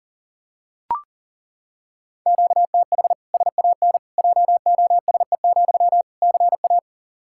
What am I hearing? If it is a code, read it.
Morse code, heard as a keyed tone, about 40 wpm, 700 Hz. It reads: QTH SAN JOSE, CA
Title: Practice QSO - Exchange 1 - Beginning - 40wpm